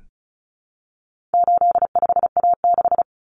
Morse code, heard as a keyed tone, about 34 wpm, 700 Hz. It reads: 85A6
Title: Common Cut Numbers - 4-Digit 35wpm